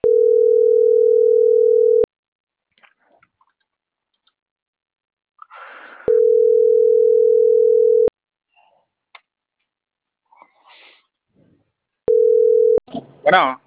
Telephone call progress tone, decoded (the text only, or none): ringback tone